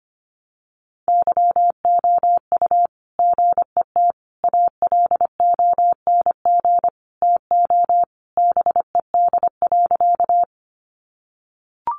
Morse code, heard as a keyed tone, about 25 wpm, 700 Hz. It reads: YOU GET ALONG TO BED.
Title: Sentences from Top 700 Words 25wpm